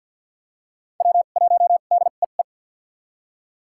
Morse code, heard as a keyed tone, about 53 wpm, spaced as for 37 wpm, 700 Hz. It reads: W1DEE
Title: USA Call Signs - 5 Characters - 2x Character Spacing - 50wpm